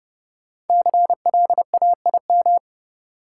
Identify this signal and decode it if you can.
Morse code, heard as a keyed tone, about 31 wpm, 700 Hz. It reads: CLAIM
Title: Sentences from Top 1500 Words - New Word Drill Alt - 30wpm